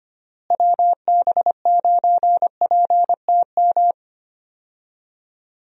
Morse code, heard as a keyed tone, about 25 wpm, 700 Hz. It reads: WB9PTM
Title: USA Call Signs - 6 Characters - Repeated 3x - 25wpm